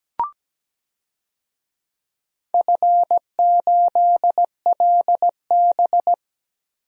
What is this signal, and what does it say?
Morse code, heard as a keyed tone, about 17 wpm, 700 Hz. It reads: F8LB
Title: DXCC-100 Call Signs - 4 Characters - Repeated 2x - 17wpm